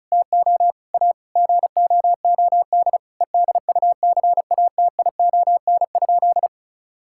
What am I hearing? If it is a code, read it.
Morse code, heard as a keyed tone, about 34 wpm, 700 Hz. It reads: TO A GOOD EDUCATION?
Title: Sentences from Top 1000 Words - Review Part 6 - 35wpm